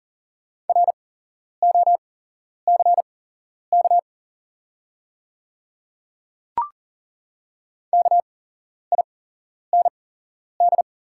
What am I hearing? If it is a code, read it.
Morse code, heard as a keyed tone, about 40 wpm, spaced as for 11 wpm, 690 Hz. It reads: ROCK KIND
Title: 8x Character Spacing - Top 500 Words - 40wpm